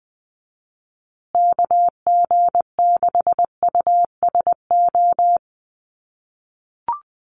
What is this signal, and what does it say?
Morse code, heard as a keyed tone, about 20 wpm, 700 Hz. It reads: KG6USO
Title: USA Call Signs Repeated 3x 20wpm